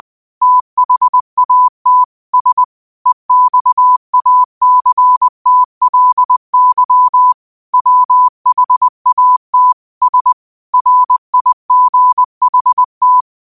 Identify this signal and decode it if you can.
Morse code, heard as a keyed tone, about 20 wpm, 1000 Hz. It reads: THAT S EXACTLY WHAT S RIGHT